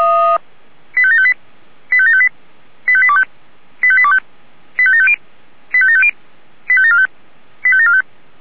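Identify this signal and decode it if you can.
Five-tone selective call (ZVEI), heard as quick runs of five tones, tones of about 70 ms. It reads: 85758 85758 85825 85825 85750 85750 85745 85745
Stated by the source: ZVEI five-tone call 85827 received on SDR.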